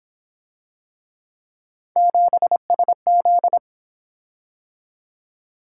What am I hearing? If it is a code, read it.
Morse code, heard as a keyed tone, about 26 wpm, 700 Hz. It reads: 7SZ
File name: Speed-Racing - Three Letter-Number Combinations - 22wpm